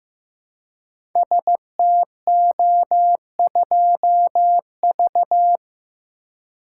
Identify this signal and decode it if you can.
Morse code, heard as a keyed tone, about 15 wpm, 700 Hz. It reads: STO2V
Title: Code Groups - 5 Letter:Numbers - 15wpm